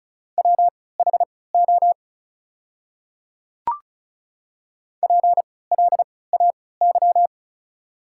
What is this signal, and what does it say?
Morse code, heard as a keyed tone, about 34 wpm, spaced as for 20 wpm, 690 Hz. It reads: WHO PLAY
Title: 3x Character Spacing - Top 200 Words - 35wpm